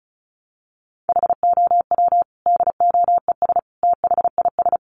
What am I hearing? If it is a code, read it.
Morse code, heard as a keyed tone, about 34 wpm, 700 Hz. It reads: HOW DOES THIS
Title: Sentences from Top 400 Words 35wpm